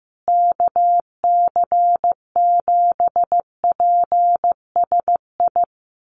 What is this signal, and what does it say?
Morse code, heard as a keyed tone, about 15 wpm, 700 Hz. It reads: KC7PSI
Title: USA Call Signs Repeated 3x 15wpm